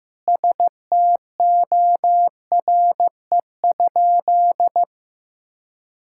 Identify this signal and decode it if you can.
Morse code, heard as a keyed tone, about 15 wpm, 700 Hz. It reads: STORE?
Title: Sentences from Top 1000 Words - Review Part 5 - 15wpm